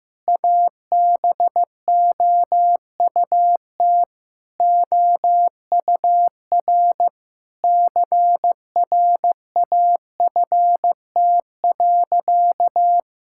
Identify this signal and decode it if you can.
Morse code, heard as a keyed tone, about 15 wpm, 700 Hz. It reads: ABOUT OUR CRAFT.
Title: Sentences from Top 1000 Words - Review Part 6 - 15wpm